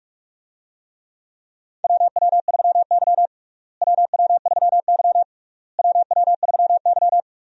Morse code, heard as a keyed tone, about 48 wpm, 700 Hz. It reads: WW3Y WW3Y WW3Y
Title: USA Call Signs - 4 Characters - Repeated 3x - 45wpm